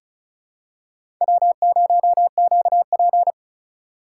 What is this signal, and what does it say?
Morse code, heard as a keyed tone, about 36 wpm, 700 Hz. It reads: W0QP
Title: USA Call Signs - 4 Characters - 3x Character Spacing - 35wpm